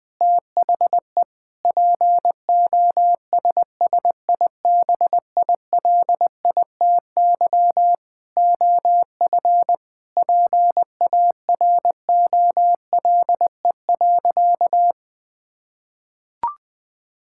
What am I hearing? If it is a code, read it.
Morse code, heard as a keyed tone, about 21 wpm, 700 Hz. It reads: THE POSSIBILITY OF PAROLE.
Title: Sentences with 8-Word Phrases - 20wpm